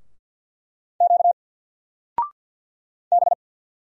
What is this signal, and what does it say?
Morse code, heard as a keyed tone, about 50 wpm, 700 Hz. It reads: Q B
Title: Single Letter-Number 50wpm